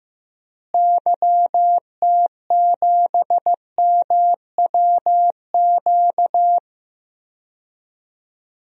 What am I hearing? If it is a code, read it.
Morse code, heard as a keyed tone, about 15 wpm, 700 Hz. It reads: YT7MWQ
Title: DXCC-100 Call Signs - 6 Characters - Repeated 3x - 15wpm